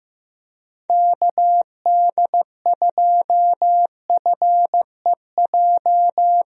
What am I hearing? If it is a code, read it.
Morse code, heard as a keyed tone, about 15 wpm, 700 Hz. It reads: KD2FEJ
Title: USA Call Signs - 6 Characters - Repeated 2x - 15wpm